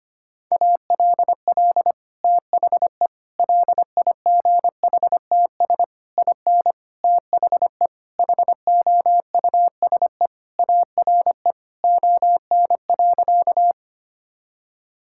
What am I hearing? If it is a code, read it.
Morse code, heard as a keyed tone, about 25 wpm, 700 Hz. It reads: ALL THE LIGHTS IN THE HOUSE ARE ON.
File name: practice / 25wpm / Sentences with 2-Word Phrases - 25wpm